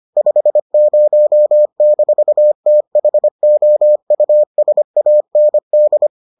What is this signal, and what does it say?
Morse code, heard as a keyed tone, about 25 wpm, 600 Hz. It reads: 50-THOUSAND